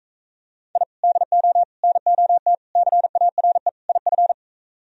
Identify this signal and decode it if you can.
Morse code, heard as a keyed tone, about 45 wpm, 700 Hz. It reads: I DO NOT CARE IF